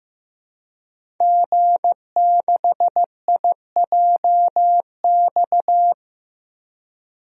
Morse code, heard as a keyed tone, about 15 wpm, 700 Hz. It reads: G6IJX
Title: DXCC-100 Call Signs - 5 Characters - 15wpm